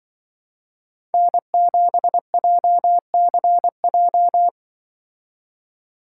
Morse code, heard as a keyed tone, about 24 wpm, 700 Hz. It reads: N7JCJ
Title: Speed Racing - USA 1x3 Call Signs 20wpm